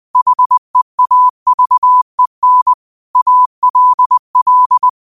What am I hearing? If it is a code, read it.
Morse code, heard as a keyed tone, about 20 wpm, 1000 Hz. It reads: HEAVEN ALL
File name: qotdfiles / 20wpm